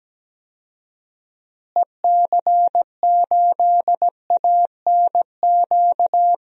Morse code, heard as a keyed tone, about 17 wpm, 700 Hz. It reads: EC8ANQ
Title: DXCC-100 Call Signs - 6 Characters - Repeated 2x - 17wpm